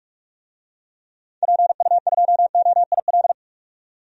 Morse code, heard as a keyed tone, about 47 wpm, 700 Hz. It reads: PU2OIL
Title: DXCC-100 Call Signs - 6 Characters - 45wpm